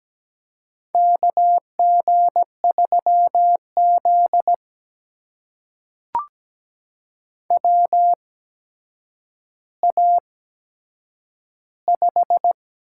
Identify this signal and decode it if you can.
Morse code, heard as a keyed tone, about 17 wpm, spaced as for 8 wpm, 700 Hz. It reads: KG3Z WA5
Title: USA Call Signs - 8x Character Spacing - 17wpm